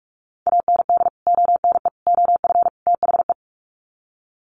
Morse code, heard as a keyed tone, about 44 wpm, 700 Hz. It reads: AND ONE OF THE
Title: Sets of 4 Words 45wpm